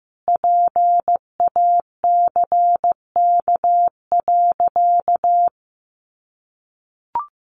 Morse code, heard as a keyed tone, about 15 wpm, 700 Hz. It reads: PACK.
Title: Sentences from Top 1300 Words 15wpm